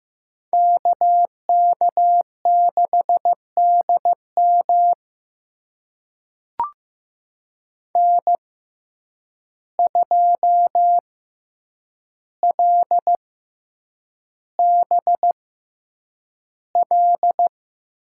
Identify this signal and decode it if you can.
Morse code, heard as a keyed tone, about 15 wpm, spaced as for 8 wpm, 700 Hz. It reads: KK6DM N2LBL
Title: USA Call Signs - 5 Characters - 6x Character Spacing - 15wpm